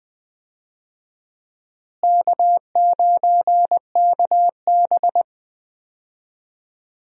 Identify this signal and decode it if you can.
Morse code, heard as a keyed tone, about 20 wpm, 700 Hz. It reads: K9KB
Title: USA Call Signs - 4 Characters - Repeated 3x - 20wpm